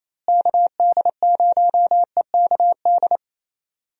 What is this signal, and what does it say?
Morse code, heard as a keyed tone, about 28 wpm, 700 Hz. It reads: KD0EKD
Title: USA Call Signs - 6 Characters - Repeated 3x - 28wpm